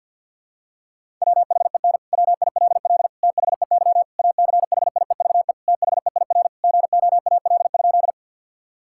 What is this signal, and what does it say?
Morse code, heard as a keyed tone, about 50 wpm, 700 Hz. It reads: WHEN WILL THEY ACHIEVE THEIR GOAL?
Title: Sentences from Top 1000 Words - Review Part 6 - 50wpm